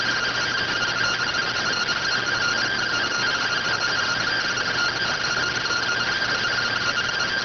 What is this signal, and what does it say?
Two-tone teleprinter signal, about 170 Hz shift.